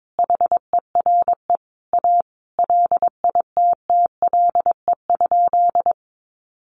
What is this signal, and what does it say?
Morse code, heard as a keyed tone, about 22 wpm, 700 Hz. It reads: HERE A LITTLE?
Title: Sentences from Top 500 Words 22wpm